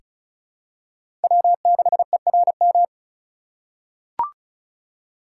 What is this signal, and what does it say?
Morse code, heard as a keyed tone, about 36 wpm, 700 Hz. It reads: W6ERM
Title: USA Call Signs - 5 Characters - 8x Character Spacing - 35wpm